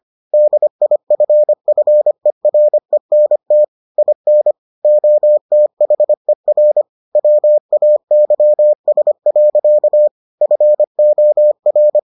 Morse code, heard as a keyed tone, about 25 wpm, 600 Hz. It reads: DIFFERENT IN OTHER WAYS. FOR